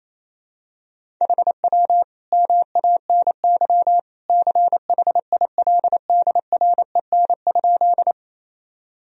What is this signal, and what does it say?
Morse code, heard as a keyed tone, about 28 wpm, 700 Hz. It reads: HW MANY CHILDREN?
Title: Sentences from Top 300 Words and Abbreviations 28wpm